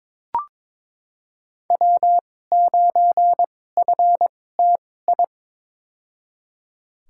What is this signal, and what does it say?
Morse code, heard as a keyed tone, about 22 wpm, spaced as for 16 wpm, 700 Hz. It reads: W9FTI
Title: USA Call Signs - 2x Character Spacing - 22wpm